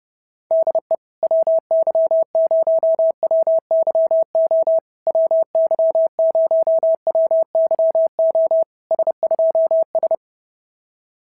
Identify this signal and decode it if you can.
Morse code, heard as a keyed tone, about 28 wpm, 650 Hz. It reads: DE WY0WYO WY0WYO S2S
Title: SOTA QSO Practice 30wpm